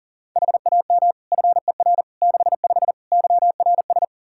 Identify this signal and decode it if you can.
Morse code, heard as a keyed tone, about 40 wpm, 700 Hz. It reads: HAM FER 65 YRS
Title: Practice QSO - Exchange 3 - More Info - 40wpm